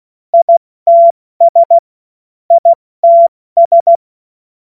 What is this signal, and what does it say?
Morse code, heard as a keyed tone, about 16 wpm, 680 Hz. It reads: ITS ITS
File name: Morsekurs / Witze / Witz 27. September 2022 70 F80